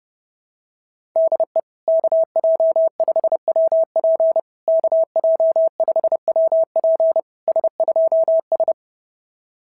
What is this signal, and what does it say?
Morse code, heard as a keyed tone, about 30 wpm, 690 Hz. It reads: DE KJ5WP KJ5WP S2S
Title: SOTA QSO Practice 30wpm